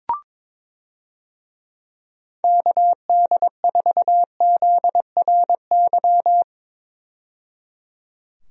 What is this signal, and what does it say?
Morse code, heard as a keyed tone, about 22 wpm, 700 Hz. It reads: KD4ZRY
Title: USA Call Signs - 6 Characters - 22wpm